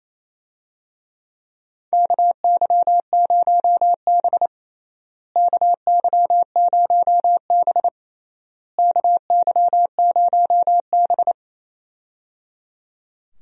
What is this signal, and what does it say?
Morse code, heard as a keyed tone, about 27 wpm, 700 Hz. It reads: KY0B KY0B KY0B
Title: USA Call Signs - 4 Characters - Repeated 3x - 28wpm